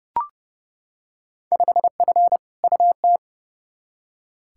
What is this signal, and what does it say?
Morse code, heard as a keyed tone, about 30 wpm, 700 Hz. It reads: 5F UT
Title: Field Day Exchanges - 30wpm